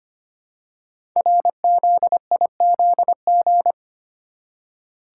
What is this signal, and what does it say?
Morse code, heard as a keyed tone, about 25 wpm, 700 Hz. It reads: RZIZG
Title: Code Groups - 5 Letters - 25wpm